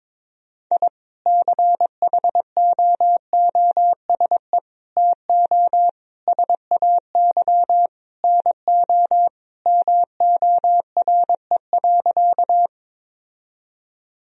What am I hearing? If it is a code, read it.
Morse code, heard as a keyed tone, about 22 wpm, 700 Hz. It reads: I CHOOSE TO SAY NO MORE.
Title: Sentences from Top 1000 Words - Review Part 4 - 22wpm